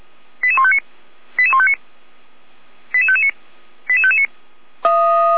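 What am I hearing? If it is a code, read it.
Five-tone selective call (ZVEI), heard as quick runs of five tones, tones of about 70 ms. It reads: 88159 88159 88559 88559
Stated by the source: Received on an SDR as a ZVEI five-tone call, address 88419.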